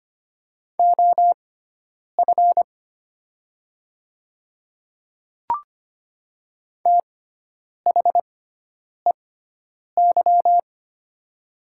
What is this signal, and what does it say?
Morse code, heard as a keyed tone, about 25 wpm, spaced as for 9 wpm, 700 Hz. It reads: OF THEY
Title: 6x Character Spacing - Top 100 Words - 25wpm